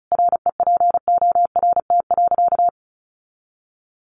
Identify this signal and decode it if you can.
Morse code, heard as a keyed tone, about 34 wpm, 700 Hz. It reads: REPORT.
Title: Sentences from Top 1300 Words 35wpm